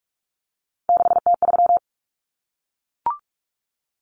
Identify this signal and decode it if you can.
Morse code, heard as a keyed tone, about 44 wpm, 700 Hz. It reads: 6T3
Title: Common Cut Numbers - 3-Digit 45wpm